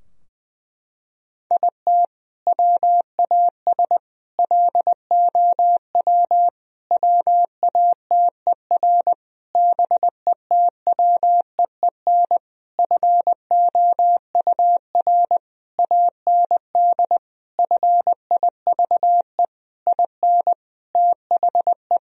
Morse code, heard as a keyed tone, about 20 wpm, 700 Hz. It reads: IT WAS LOW WATER BETWEEN FOUR AND FIVE IN THE
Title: Sentences from Top 1000 Words - Review Part 2 - 20wpm